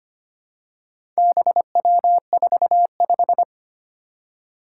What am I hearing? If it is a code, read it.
Morse code, heard as a keyed tone, about 25 wpm, 700 Hz. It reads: BW45
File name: practice / 25wpm / Four Letter-Number Combinations - 25wpm